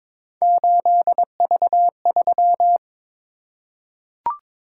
Morse code, heard as a keyed tone, about 23 wpm, 700 Hz. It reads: 8V3
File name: practice / 22wpm / Three Letter-Number Combinations - 22wpm